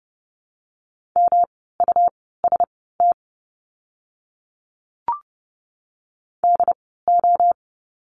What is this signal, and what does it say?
Morse code, heard as a keyed tone, about 30 wpm, spaced as for 17 wpm, 700 Hz. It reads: MUST DO
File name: course / 30:10wpm / 237 Lesson 38f - Letter q - Words 30wpm